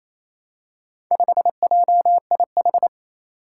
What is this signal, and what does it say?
Morse code, heard as a keyed tone, about 28 wpm, 700 Hz. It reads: 5JIH
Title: Four Challenging Letter-Number Combinations 28wpm